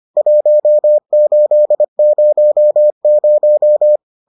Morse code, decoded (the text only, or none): 1800